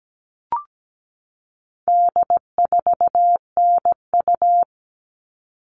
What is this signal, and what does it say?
Morse code, heard as a keyed tone, about 17 wpm, 700 Hz. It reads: D4NU
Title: All Cut Numbers - 4-Digit 17wpm